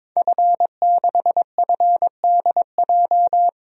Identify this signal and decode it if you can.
Morse code, heard as a keyed tone, about 22 wpm, 700 Hz. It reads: F6FDJ
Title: DXCC-100 Call Signs - 5 Characters - 22wpm